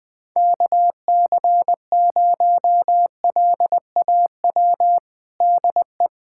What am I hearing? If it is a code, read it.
Morse code, heard as a keyed tone, about 20 wpm, 700 Hz. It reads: KC0LAW DE
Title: USA Callsign DE Callsign - 6 Characters - 20wpm